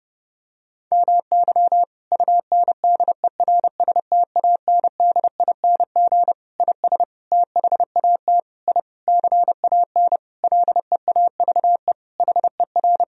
Morse code, heard as a keyed tone, about 30 wpm, 700 Hz. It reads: MY UNDERSTANDING IS THAT I CAN LEAVE HER
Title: Sentences from Top 1000 Words - Review Part 3 - 30wpm